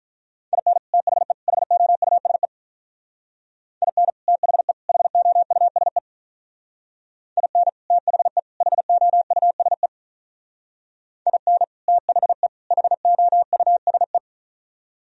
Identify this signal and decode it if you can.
Morse code, heard as a keyed tone, about 44 wpm, 700 Hz. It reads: IN THE HOUSE IN THE HOUSE IN THE HOUSE IN THE HOUSE